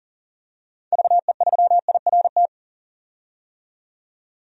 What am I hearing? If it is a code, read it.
Morse code, heard as a keyed tone, about 40 wpm, 700 Hz. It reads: VE3IRT
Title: DXCC-100 Call Signs - 6 Characters - 40wpm